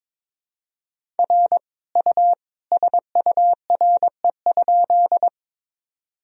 Morse code, heard as a keyed tone, about 21 wpm, 700 Hz. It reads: R U SURE?
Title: Sentences from Top 300 Words and Abbreviations 22wpm